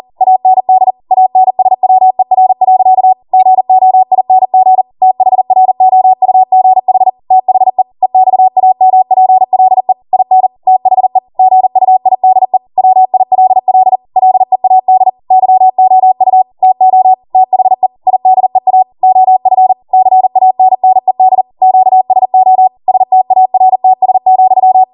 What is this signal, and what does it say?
Morse code, heard as a keyed tone, about 40 wpm, 750 Hz. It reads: AND ANSWER. GOING THROUGH THE EXAMPLE IN THE GUIDE WILL LEAD YOU TO THE IDEA OF CANNED QSO STARTS,